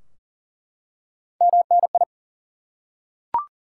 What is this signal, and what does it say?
Morse code, heard as a keyed tone, about 40 wpm, 700 Hz. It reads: MNI
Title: Top 90 QSO Elements 40wpm